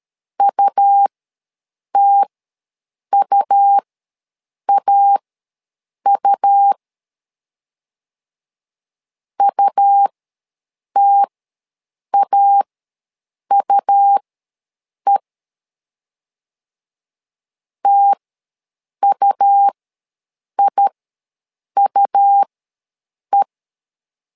Morse code, heard as a keyed tone, about 13 wpm, spaced as for 7 wpm, 800 Hz. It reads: UTUAU UTAUE TUIUE
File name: LEZIONI / 2 UHDN / 2 U ws 12wpm 1655